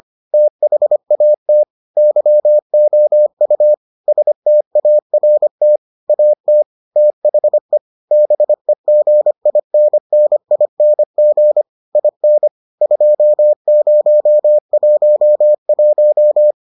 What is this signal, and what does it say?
Morse code, heard as a keyed tone, about 25 wpm, 600 Hz. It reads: THAT YOU START AT THE BEGINNING IN 2011